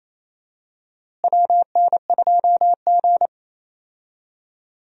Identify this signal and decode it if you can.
Morse code, heard as a keyed tone, about 27 wpm, 700 Hz. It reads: WN2G
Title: USA Call Signs - 4 Characters - Repeated 2x - 28wpm